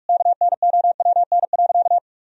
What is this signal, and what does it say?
Morse code, heard as a keyed tone, about 45 wpm, 700 Hz. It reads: KNOWN.
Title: Sentences from Top 1000 Words - Review Part 4 - 45wpm